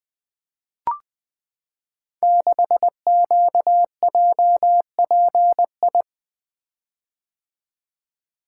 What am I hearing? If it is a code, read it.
Morse code, heard as a keyed tone, about 20 wpm, 700 Hz. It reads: 6QJPI
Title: Code Groups - 5 Letter:Numbers - 20wpm